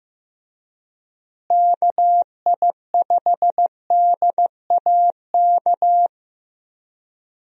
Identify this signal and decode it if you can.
Morse code, heard as a keyed tone, about 15 wpm, 700 Hz. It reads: KI5DAK
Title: USA Call Signs - 6 Characters - Repeated 3x - 15wpm